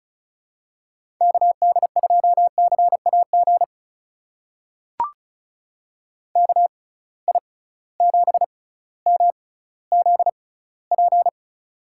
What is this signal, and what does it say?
Morse code, heard as a keyed tone, about 34 wpm, spaced as for 18 wpm, 700 Hz. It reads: KD2CAG KI7MZP
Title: USA Call Signs - 6 Characters - 6x Character Spacing - 35wpm